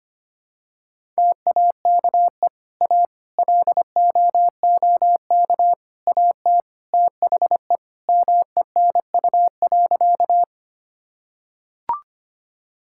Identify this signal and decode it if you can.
Morse code, heard as a keyed tone, about 25 wpm, 700 Hz. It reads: TAKE A LOOK AT THE MENU.
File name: practice / 25wpm / Sentences with 4-Word Phrases - 25wpm